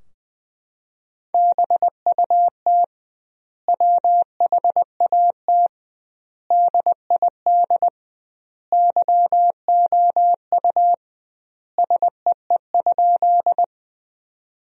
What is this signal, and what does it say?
Morse code, heard as a keyed tone, about 20 wpm, 700 Hz. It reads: BUT WHAT DID YOU SEE?
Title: 2x Word-Spacing - Sentences from Top 100 Words - 20wpm